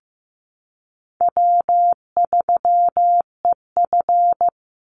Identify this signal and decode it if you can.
Morse code, heard as a keyed tone, about 15 wpm, 700 Hz. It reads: W3EF
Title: Call Signs Repeated 3x - 15wpm